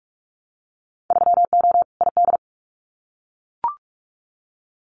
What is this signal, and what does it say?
Morse code, heard as a keyed tone, about 44 wpm, 700 Hz. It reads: 3O ID
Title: Winter Field Day Exchanges 45wpm